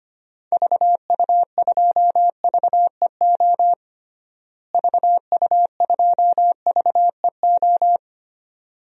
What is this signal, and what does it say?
Morse code, heard as a keyed tone, about 25 wpm, 700 Hz. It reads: VU2VEO VU2VEO
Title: DXCC-100 Call Signs - 6 Characters - Repeated 2x - 25wpm